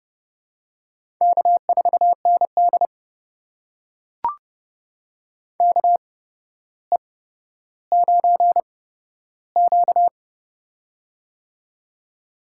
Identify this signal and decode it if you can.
Morse code, heard as a keyed tone, about 30 wpm, spaced as for 13 wpm, 700 Hz. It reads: K4ND KE9Q
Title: USA Call Signs - 4 Characters - 8x Character Spacing - 30wpm